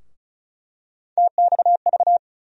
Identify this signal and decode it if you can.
Morse code, heard as a keyed tone, about 34 wpm, 700 Hz. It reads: TXV
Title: USA Call Sign Postfixes - 3 Letters 35wpm